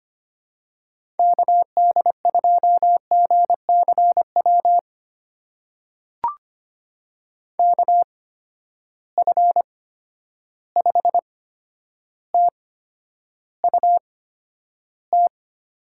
Morse code, heard as a keyed tone, about 25 wpm, spaced as for 11 wpm, 700 Hz. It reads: KD2GCW KF5TUT
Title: USA Call Signs - 6 Characters - 8x Character Spacing - 25wpm